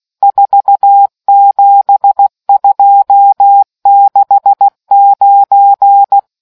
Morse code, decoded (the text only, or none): 47269